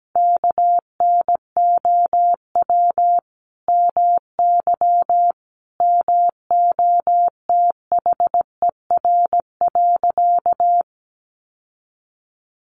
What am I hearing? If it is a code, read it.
Morse code, heard as a keyed tone, about 17 wpm, 700 Hz. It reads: KNOW MY MOTHER.
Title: Sentences from Top 600 Words 17wpm